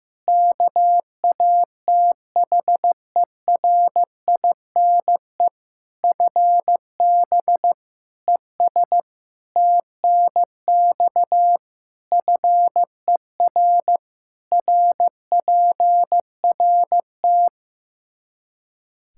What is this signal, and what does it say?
Morse code, heard as a keyed tone, about 15 wpm, 700 Hz. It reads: KATHERINE FB ES TNX FER RPRT
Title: K1USN SST QSO Practice 15wpm